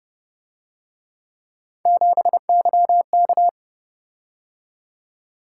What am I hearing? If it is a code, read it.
Morse code, heard as a keyed tone, about 29 wpm, 700 Hz. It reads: 7YK